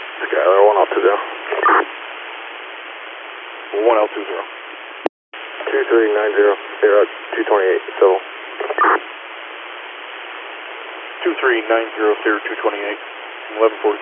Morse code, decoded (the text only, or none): none